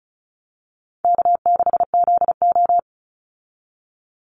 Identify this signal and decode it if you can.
Morse code, heard as a keyed tone, about 34 wpm, 700 Hz. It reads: K6ZO
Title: Easy Call Signs Repeated 2x - 35wpm